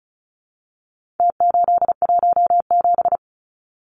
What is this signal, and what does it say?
Morse code, heard as a keyed tone, about 34 wpm, 700 Hz. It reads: T817
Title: Numbers - Contest Style - 4-Digit 35wpm